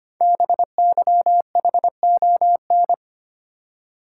Morse code, heard as a keyed tone, about 25 wpm, 700 Hz. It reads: BYHON
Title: Code Groups - 5 Letters - 25wpm